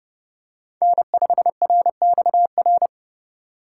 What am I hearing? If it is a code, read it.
Morse code, heard as a keyed tone, about 30 wpm, 700 Hz. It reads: N5RXR
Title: USA Call Signs - 5 Characters - 3x Character Spacing - 30wpm